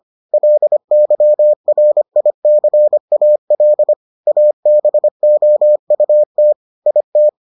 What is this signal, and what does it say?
Morse code, heard as a keyed tone, about 25 wpm, 600 Hz. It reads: LYRICAL ABOUT IT